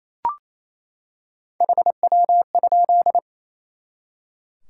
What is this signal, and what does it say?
Morse code, heard as a keyed tone, about 28 wpm, 700 Hz. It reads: HW?
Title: Top 180 QSO Elements 28wpm